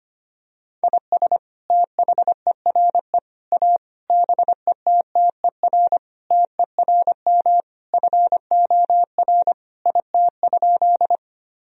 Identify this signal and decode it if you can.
Morse code, heard as a keyed tone, about 25 wpm, 690 Hz. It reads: IS THERE A BETTER TERM FOR IT?
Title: Sentences from Top 1000 Words - Review Part 4 - 25wpm